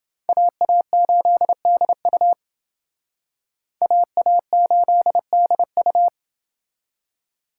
Morse code, heard as a keyed tone, about 30 wpm, 700 Hz. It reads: AA8DU AA8DU